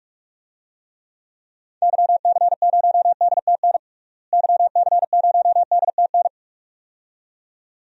Morse code, heard as a keyed tone, about 43 wpm, 700 Hz. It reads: YC0DTN YC0DTN
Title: DXCC-100 Call Signs - 6 Characters - Repeated 2x - 45wpm